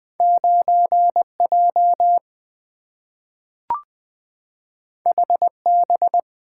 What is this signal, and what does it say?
Morse code, heard as a keyed tone, about 20 wpm, 700 Hz. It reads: 9J HB
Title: Two Letter-Number Combinations - 20wpm